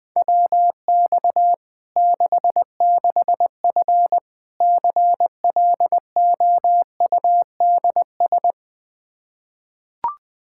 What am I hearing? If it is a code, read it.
Morse code, heard as a keyed tone, about 20 wpm, 700 Hz. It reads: WX 66F CLOUDS
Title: Practice QSO - Exchange 2 - Info - 20wpm